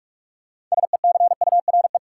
Morse code, heard as a keyed tone, about 45 wpm, 700 Hz. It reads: SECURE
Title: Sentences from Top 1200 Words - New Word Drill Alt - 45wpm